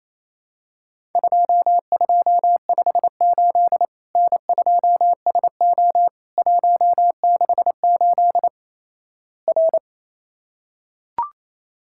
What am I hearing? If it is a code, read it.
Morse code, heard as a keyed tone, about 27 wpm, 700 Hz. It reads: 2258 N2SO 168 R
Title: Worked All Europe DX Contest - QTC Exchange - 28wpm